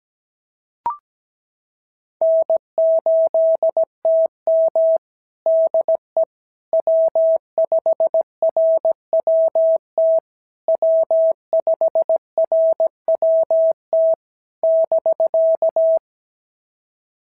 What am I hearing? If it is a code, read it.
Morse code, heard as a keyed tone, about 17 wpm, 650 Hz. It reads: N8TM DE W5RWT W5RWT <BK>
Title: SOTA QSO Practice 17wpm